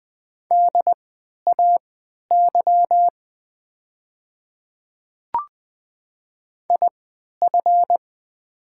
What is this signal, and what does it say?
Morse code, heard as a keyed tone, about 20 wpm, spaced as for 11 wpm, 700 Hz. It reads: DAY IF
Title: 3x Character Spacing - Top 300 Words - 20wpm